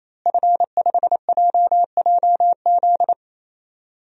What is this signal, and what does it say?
Morse code, heard as a keyed tone, about 27 wpm, 700 Hz. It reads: F5JJZ
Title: DXCC-100 Call Signs - 5 Characters - Repeated 3x - 28wpm